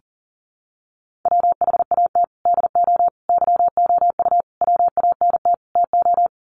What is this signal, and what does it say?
Morse code, heard as a keyed tone, about 40 wpm, 700 Hz. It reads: WHAT DO YOU WANT TO
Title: Sets of 5 Words 40wpm